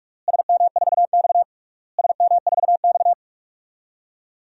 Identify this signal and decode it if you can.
Morse code, heard as a keyed tone, about 47 wpm, 690 Hz. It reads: SM4X SM4X
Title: DXCC-100 Call Signs - 4 Characters - Repeated 3x - 45wpm